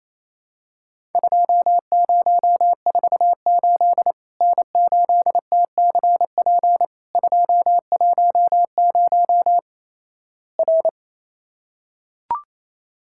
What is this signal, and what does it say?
Morse code, heard as a keyed tone, about 28 wpm, 700 Hz. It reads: 2048 N8TCP 210 R
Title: Worked All Europe DX Contest - QTC Exchange - 28wpm